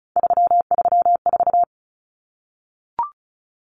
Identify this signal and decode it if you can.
Morse code, heard as a keyed tone, about 34 wpm, 700 Hz. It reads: 334
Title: Common Cut Numbers - 3-Digit 35wpm